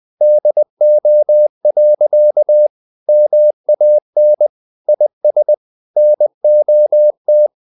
Morse code, heard as a keyed tone, about 20 wpm, 600 Hz. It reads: DO. MAN IS NOT
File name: the old man and the sea / the old man and the sea part 1 20wpm